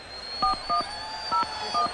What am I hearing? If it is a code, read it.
DTMF, heard as telephone touch tones, tones of about 130 ms. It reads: 4101